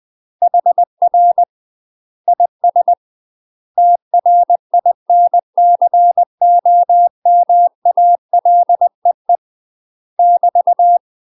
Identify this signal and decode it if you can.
Morse code, heard as a keyed tone, about 20 wpm, 700 Hz. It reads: HR IS TRINCOMALEE <BT>
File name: UK MASTER QSOs TRAINING / 20WPM / UK MASTER QSOs 0050